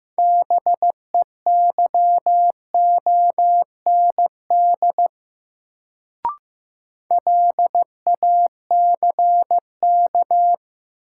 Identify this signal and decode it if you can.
Morse code, heard as a keyed tone, about 15 wpm, 700 Hz. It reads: BEYOND LACK
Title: Sentences from Top 1100 Words - New Word Drill Alt - 15wpm